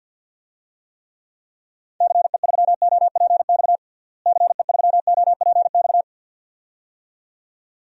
Morse code, heard as a keyed tone, about 47 wpm, 700 Hz. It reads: CE3OPX CE3OPX
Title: DXCC-100 Call Signs - 6 Characters - Repeated 2x - 50wpm